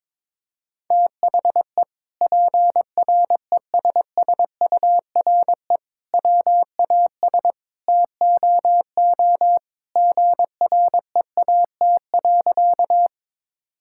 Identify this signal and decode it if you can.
Morse code, heard as a keyed tone, about 22 wpm, 700 Hz. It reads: THE PRESSURE WAS TOO GREAT.
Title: Sentences from Top 1000 Words - Review Part 6 - 22wpm